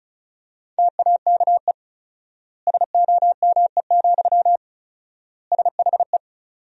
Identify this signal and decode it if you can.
Morse code, heard as a keyed tone, about 36 wpm, 700 Hz. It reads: TAKE SOME, SHE
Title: 4x Word-Spacing - Sentences from Top 200 Words - 35wpm